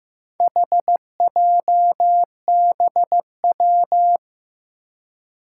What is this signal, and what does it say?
Morse code, heard as a keyed tone, about 15 wpm, 700 Hz. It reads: HJBW
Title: Four Challenging Letter-Number Combinations 15wpm